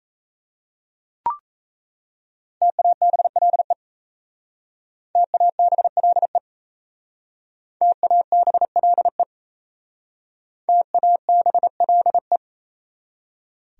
Morse code, heard as a keyed tone, about 34 wpm, 700 Hz. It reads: TABLE TABLE TABLE TABLE
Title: Speed-Racing Top 500 Words 28wpm